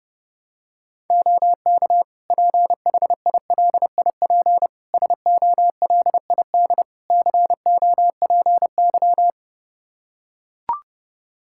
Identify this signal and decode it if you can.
Morse code, heard as a keyed tone, about 32 wpm, 700 Hz. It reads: OK PHILIP SOLID COPY
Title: K1USN SST QSO Practice 30wpm